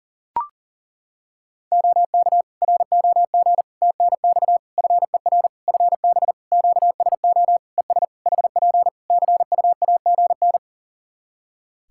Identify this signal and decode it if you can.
Morse code, heard as a keyed tone, about 40 wpm, 700 Hz. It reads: OK ROG TNX FER FB QSO ES HP CUAGN
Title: Practice QSO - Exchange 4 - Ending - 40wpm